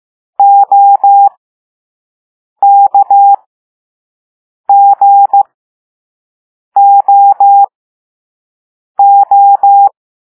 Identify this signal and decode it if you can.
Morse code, heard as a keyed tone, about 15 wpm, spaced as for 5 wpm, 800 Hz. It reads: OKGOO
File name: Morse / Level4A